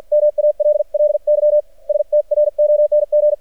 Morse code, heard as a keyed tone, about 56 wpm, spaced as for 47 wpm, 600 Hz. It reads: MALLY STRONG
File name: archived / CRSnet